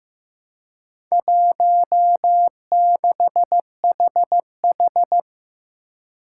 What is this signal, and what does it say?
Morse code, heard as a keyed tone, about 15 wpm, 700 Hz. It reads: 16HH
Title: Four Challenging Letter-Number Combinations 15wpm